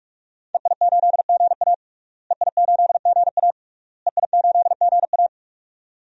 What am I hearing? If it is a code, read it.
Morse code, heard as a keyed tone, about 42 wpm, 700 Hz. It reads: EI8GA EI8GA EI8GA
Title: DXCC-100 Call Signs - 5 Characters - Repeated 3x - 45wpm